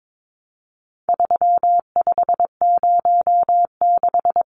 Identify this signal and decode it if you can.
Morse code, heard as a keyed tone, about 22 wpm, 700 Hz. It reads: 3506
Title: Numbers - 4-Digit 22wpm